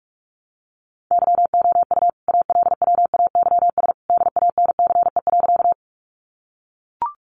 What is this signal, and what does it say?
Morse code, heard as a keyed tone, about 44 wpm, 700 Hz. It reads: YOU ALWAYS DANCE.
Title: Sentences from Top 700 Words 45wpm